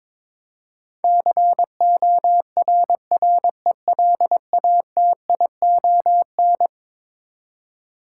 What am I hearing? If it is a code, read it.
Morse code, heard as a keyed tone, about 22 wpm, 700 Hz. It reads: CORRELATION
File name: practice / 22wpm / Eleven Letter Words - 22wpm